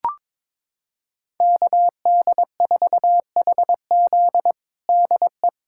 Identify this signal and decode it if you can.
Morse code, heard as a keyed tone, about 22 wpm, 700 Hz. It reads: KD4HZ DE
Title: USA Callsign DE Callsign - 5 Characters - 22wpm